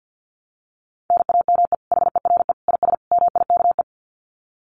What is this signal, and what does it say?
Morse code, heard as a keyed tone, about 50 wpm, 700 Hz. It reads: NAME HERE IS MIKE
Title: Common QSO Phrases 50wpm